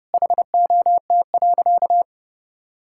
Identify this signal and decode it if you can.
Morse code, heard as a keyed tone, about 32 wpm, 700 Hz. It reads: HOT.
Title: Sentences from Top 1000 Words - Review Part 6 - 30wpm